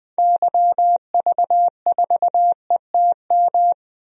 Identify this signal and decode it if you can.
Morse code, heard as a keyed tone, about 20 wpm, 700 Hz. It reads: YV4ETM
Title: DXCC-100 Call Signs - 6 Characters - Repeated 3x - 20wpm